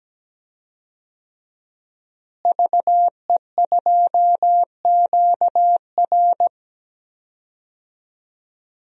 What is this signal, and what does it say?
Morse code, heard as a keyed tone, about 17 wpm, 700 Hz. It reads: VE2QR
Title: DXCC-100 Call Signs - 5 Characters - Repeated 3x - 17wpm